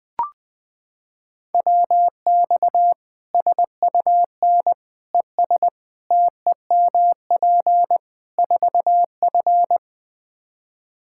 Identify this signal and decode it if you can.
Morse code, heard as a keyed tone, about 20 wpm, 700 Hz. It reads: WX SUN ES TEMP 4F
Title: Practice QSO - Exchange 2 - Info - 20wpm